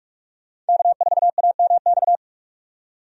Morse code, heard as a keyed tone, about 46 wpm, 700 Hz. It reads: K4AMX
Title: USA Call Signs - 5 Characters - 45wpm